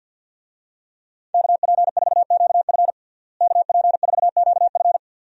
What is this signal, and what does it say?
Morse code, heard as a keyed tone, about 52 wpm, 700 Hz. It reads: KP4QF KP4QF
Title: DXCC-100 Call Signs - 5 Characters - Repeated 2x - 50wpm